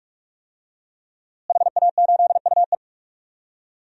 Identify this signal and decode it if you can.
Morse code, heard as a keyed tone, about 46 wpm, 700 Hz. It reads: HA8UE
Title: DXCC-100 Call Signs - 5 Characters - Repeated 2x - 45wpm